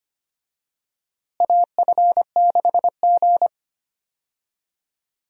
Morse code, heard as a keyed tone, about 25 wpm, 700 Hz. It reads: AF6G
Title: USA Call Signs - 4 Characters - Repeated 2x - 25wpm